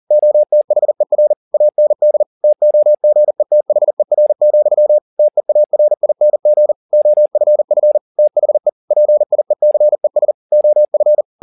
Morse code, heard as a keyed tone, about 40 wpm, 600 Hz. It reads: OTHER AND TOGETHER, TEARING OFF THE PIECES OF
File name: the old man and the sea 2 40wpm